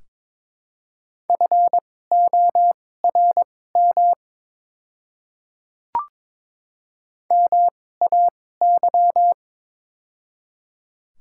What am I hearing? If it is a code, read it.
Morse code, heard as a keyed tone, about 21 wpm, spaced as for 16 wpm, 700 Hz. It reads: FORM MAY
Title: 2x Character Spacing - Top 300 Words - 22wpm